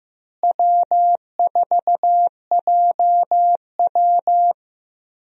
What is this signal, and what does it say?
Morse code, heard as a keyed tone, about 15 wpm, 700 Hz. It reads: W4JW
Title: USA Call Signs - 4 Characters - Repeated 3x - 15wpm